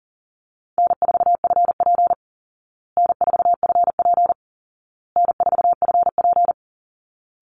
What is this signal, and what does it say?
Morse code, heard as a keyed tone, about 40 wpm, 700 Hz. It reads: N4FP N4FP N4FP
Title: Call Signs Repeated 2x - 40wpm